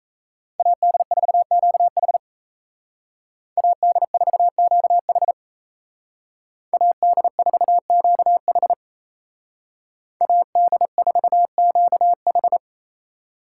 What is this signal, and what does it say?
Morse code, heard as a keyed tone, about 37 wpm, 700 Hz. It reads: AD4QH AD4QH AD4QH AD4QH